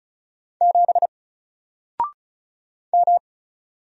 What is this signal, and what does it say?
Morse code, heard as a keyed tone, about 34 wpm, 700 Hz. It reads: 7 M
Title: Single Letter-Number 35wpm